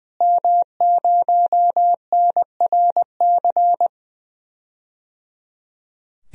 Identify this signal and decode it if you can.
Morse code, heard as a keyed tone, about 21 wpm, 700 Hz. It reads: M0NRC
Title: DXCC-100 Call Signs - 5 Characters - Repeated 3x - 20wpm